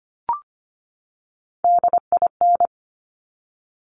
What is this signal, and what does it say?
Morse code, heard as a keyed tone, about 25 wpm, 700 Hz. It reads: DIN
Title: Top Three Letter Combinations in Words 25wpm